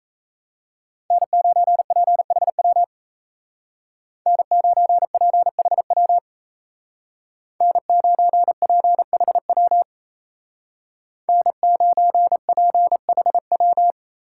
Morse code, heard as a keyed tone, about 35 wpm, 700 Hz. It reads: N9PHW N9PHW N9PHW N9PHW